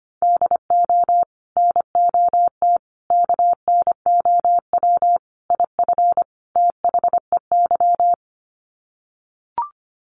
Morse code, heard as a keyed tone, about 25 wpm, 700 Hz. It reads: DO NOT KNOW IF THEY
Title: Sets of 5 Words Encore 25wpm